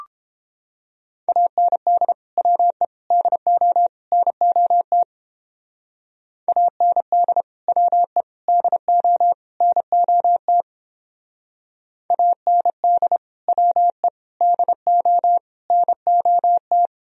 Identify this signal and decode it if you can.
Morse code, heard as a keyed tone, about 30 wpm, 700 Hz. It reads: AND WE DO NOT AND WE DO NOT AND WE DO NOT